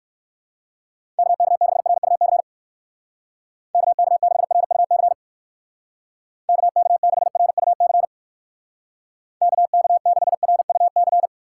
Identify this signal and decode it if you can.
Morse code, heard as a keyed tone, about 59 wpm, 700 Hz. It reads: KK6RUC KK6RUC KK6RUC KK6RUC